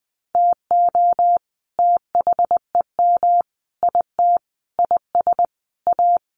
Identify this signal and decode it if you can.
Morse code, heard as a keyed tone, about 20 wpm, 700 Hz. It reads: TO THEM IT IS A
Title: Sentences from Top 1400 Words 20wpm